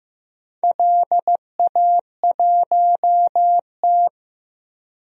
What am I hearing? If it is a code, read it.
Morse code, heard as a keyed tone, about 15 wpm, 700 Hz. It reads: LA1T
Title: DXCC-100 Call Signs - 4 Characters - 15wpm